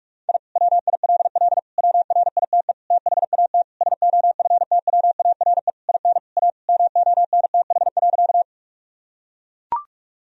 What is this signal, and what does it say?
Morse code, heard as a keyed tone, about 41 wpm, 700 Hz. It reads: I WILL WRITE THAT SOFTWARE IN A MONTH.
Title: Sentences from Top 1000 Words - Review Part 3 - 45wpm